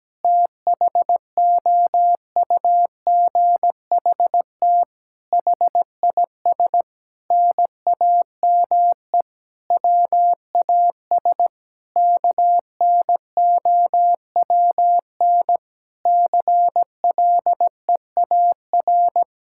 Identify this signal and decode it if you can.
Morse code, heard as a keyed tone, about 17 wpm, 700 Hz. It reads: THOUGHT HIS NAME WAS KNOWN CLEAR
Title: Sentences from Top 1000 Words - Review Part 5 - 17wpm